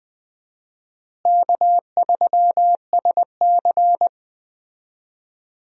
Morse code, heard as a keyed tone, about 20 wpm, 700 Hz. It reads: K3SC
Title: USA Call Signs - 4 Characters - Repeated 2x - 20wpm